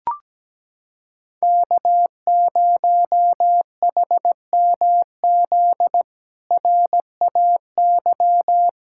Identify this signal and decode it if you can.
Morse code, heard as a keyed tone, about 17 wpm, 700 Hz. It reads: K0HMZ RAY
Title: K1USN SST QSO Practice - 17wpm